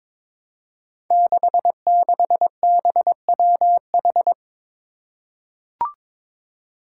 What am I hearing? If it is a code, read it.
Morse code, heard as a keyed tone, about 22 wpm, 700 Hz. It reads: 66BWH
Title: Five Challenging Letter-Number Combinations 22wpm